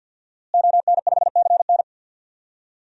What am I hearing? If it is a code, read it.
Morse code, heard as a keyed tone, about 53 wpm, 700 Hz. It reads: ON5CN
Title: DXCC-100 Call Signs - 5 Characters - Repeated 3x - 50wpm